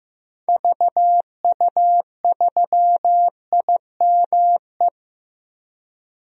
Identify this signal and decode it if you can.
Morse code, heard as a keyed tone, about 15 wpm, 700 Hz. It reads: VU3IME
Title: DXCC-100 Call Signs - 6 Characters - Repeated 3x - 15wpm